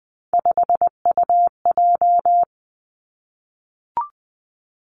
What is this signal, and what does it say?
Morse code, heard as a keyed tone, about 20 wpm, 700 Hz. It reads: 5UJ
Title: Three Challenging Letter-Number Combinations 20wpm